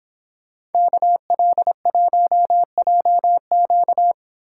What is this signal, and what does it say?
Morse code, heard as a keyed tone, about 30 wpm, 700 Hz. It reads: KL1JQ